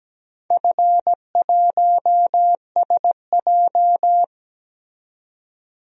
Morse code, heard as a keyed tone, about 17 wpm, 700 Hz. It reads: F1SJ
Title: DXCC-100 Call Signs - 4 Characters - 17wpm